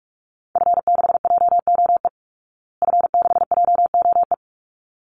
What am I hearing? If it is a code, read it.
Morse code, heard as a keyed tone, about 44 wpm, 700 Hz. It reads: F6JOE F6JOE
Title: Call Signs Repeated 3x - 45wpm